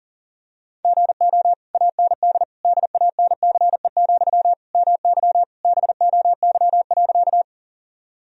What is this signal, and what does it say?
Morse code, heard as a keyed tone, about 40 wpm, 700 Hz. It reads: GO AND DANCE, MY BOY.
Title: Sentences from Top 1000 Words - Review Part 4 - 40wpm